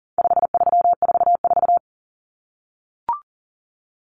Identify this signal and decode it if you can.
Morse code, heard as a keyed tone, about 40 wpm, 700 Hz. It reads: 5344
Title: Numbers - 4-Digit 40wpm